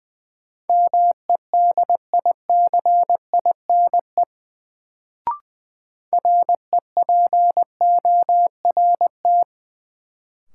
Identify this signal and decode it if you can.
Morse code, heard as a keyed tone, about 21 wpm, 700 Hz. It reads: MEDICINE REPORT
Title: Sentences from Top 900 Words - New Word Drill - 20wpm